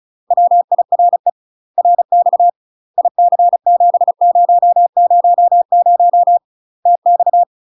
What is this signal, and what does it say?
Morse code, heard as a keyed tone, about 34 wpm, 700 Hz. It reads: WIRE RX IC7000 TX
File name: UK MASTER QSOs TRAINING / 35WPM / UK MASTER QSOs 0041